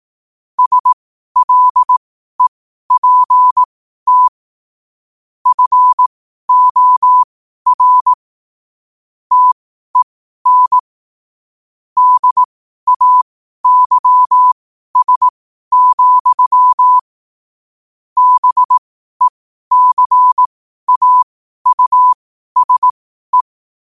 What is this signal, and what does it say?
Morse code, heard as a keyed tone, about 18 wpm, spaced as for 12 wpm, 1000 Hz. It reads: SLEPT FOR TEN DAYS, BECAUSE
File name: qotdfiles / 12wpm